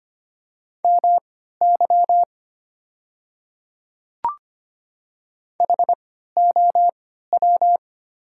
Morse code, heard as a keyed tone, about 25 wpm, spaced as for 14 wpm, 700 Hz. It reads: MY HOW
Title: 3x Character Spacing - Top 100 Words - 25wpm